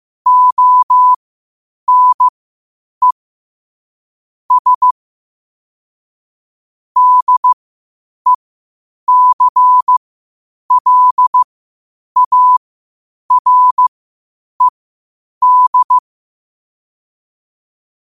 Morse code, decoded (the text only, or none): ONE S DECLARED